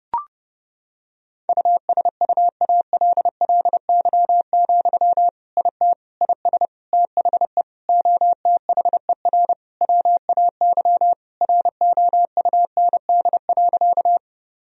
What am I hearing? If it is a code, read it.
Morse code, heard as a keyed tone, about 30 wpm, 700 Hz. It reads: USUALLY, IT IS THE OTHER WAY ROUND.
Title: Sentences from Top 1000 Words - Review Part 3 - 30wpm